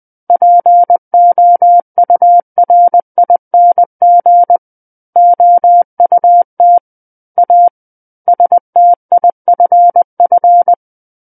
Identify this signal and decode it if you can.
Morse code, heard as a keyed tone, about 20 wpm, 700 Hz. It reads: POURING OUT A STIFF
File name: 20 WPM